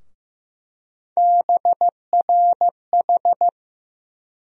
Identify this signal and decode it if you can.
Morse code, heard as a keyed tone, about 15 wpm, 700 Hz. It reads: BRH
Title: USA Call Sign Postfixes - 3 Letters 15wpm